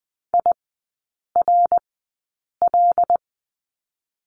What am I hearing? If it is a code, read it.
Morse code, heard as a keyed tone, about 20 wpm, spaced as for 8 wpm, 700 Hz. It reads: IRL
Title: 127 Lesson 21e - Number 6 - Three Characters 20wpm